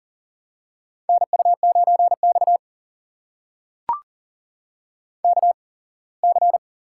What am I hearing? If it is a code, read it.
Morse code, heard as a keyed tone, about 42 wpm, spaced as for 24 wpm, 700 Hz. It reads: NU9X KC
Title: USA Call Signs - 8x Character Spacing - 40wpm